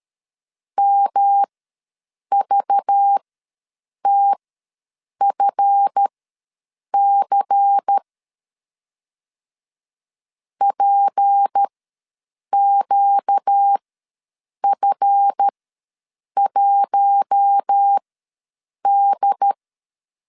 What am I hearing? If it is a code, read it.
Morse code, heard as a keyed tone, about 13 wpm, spaced as for 7 wpm, 800 Hz. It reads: MVTFC PQF1D
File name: LEZIONI / 7 GF9Z / 7 GF ws 12wpm 3140